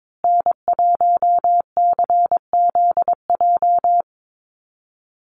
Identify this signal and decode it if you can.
Morse code, heard as a keyed tone, about 22 wpm, 700 Hz. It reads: N1CZJ
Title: USA Call Signs Repeated 3x 22wpm